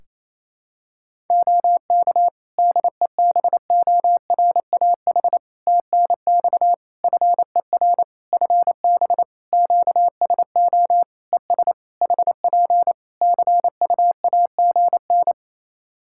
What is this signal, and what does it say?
Morse code, heard as a keyed tone, about 28 wpm, 700 Hz. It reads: OK DEBORAH TNX FER FB QSO ES HP CUAGN
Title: Practice QSO - Exchange 4 - Ending - 28wpm